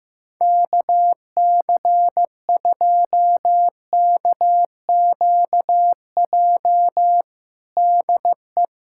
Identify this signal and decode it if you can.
Morse code, heard as a keyed tone, about 15 wpm, 700 Hz. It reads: KC2KQJ DE
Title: USA Callsign DE Callsign - 6 Characters - 15wpm